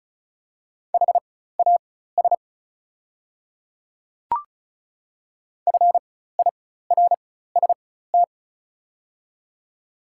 Morse code, heard as a keyed tone, about 36 wpm, spaced as for 16 wpm, 700 Hz. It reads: HAS FIRST
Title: 4x Character Spacing - Top 100 Words - 35wpm